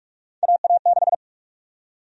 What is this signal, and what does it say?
Morse code, heard as a keyed tone, about 46 wpm, 700 Hz. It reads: AA6
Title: USA Call Sign Prefixes - 45wpm